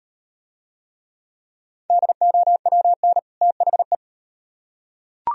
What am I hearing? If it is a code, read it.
Morse code, heard as a keyed tone, about 35 wpm, 700 Hz. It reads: DOWN THE